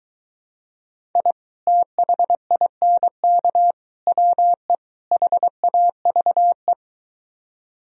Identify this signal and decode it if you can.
Morse code, heard as a keyed tone, about 23 wpm, 700 Hz. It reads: I THINK WE HAVE